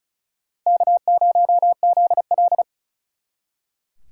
Morse code, heard as a keyed tone, about 46 wpm, 700 Hz. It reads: K0ZL